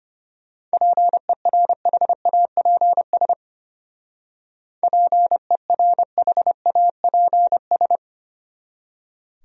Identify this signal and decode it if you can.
Morse code, heard as a keyed tone, about 31 wpm, 700 Hz. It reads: PERHAPS PERHAPS
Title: Speed-Racing Top 500 Words 25wpm